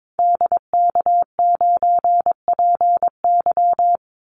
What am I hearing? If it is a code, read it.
Morse code, heard as a keyed tone, about 22 wpm, 700 Hz. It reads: DK9PY
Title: Call Signs - 22wpm